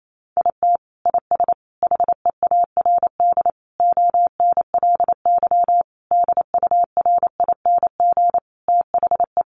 Morse code, heard as a keyed tone, about 28 wpm, 700 Hz. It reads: IT IS HEARD ONLY DURING THE
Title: Sentences from Top 1200 Words 28wpm